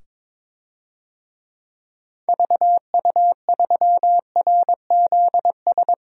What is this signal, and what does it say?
Morse code, heard as a keyed tone, about 22 wpm, 700 Hz. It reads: VU3RZS
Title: DXCC-100 Call Signs - 6 Characters - 22wpm